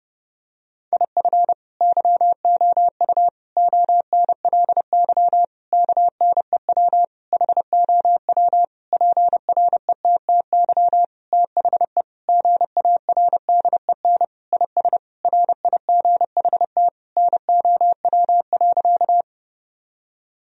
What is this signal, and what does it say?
Morse code, heard as a keyed tone, about 30 wpm, 700 Hz. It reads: IF YOU ONLY KNEW HOW PRETTY THE GARDEN IS RIGHT NOW.
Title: Sentences from Top 1000 Words - Review Part 4 - 30wpm